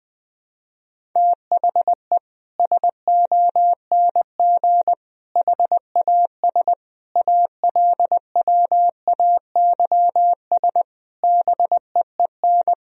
Morse code, heard as a keyed tone, about 20 wpm, 700 Hz. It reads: THE SONG HAS ALWAYS BEEN
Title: Sentences from Top 1000 Words - Review Part 3 - 20wpm